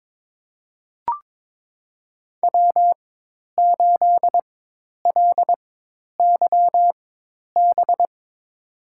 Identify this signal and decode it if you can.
Morse code, heard as a keyed tone, about 23 wpm, spaced as for 10 wpm, 700 Hz. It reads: W8LYB
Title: USA Call Signs - 5 Characters - 4x Character Spacing - 22wpm